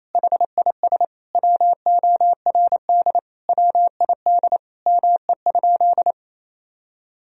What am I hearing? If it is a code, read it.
Morse code, heard as a keyed tone, about 28 wpm, 700 Hz. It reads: HIS WORD WID ME?
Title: Sentences from Top 300 Words and Abbreviations 28wpm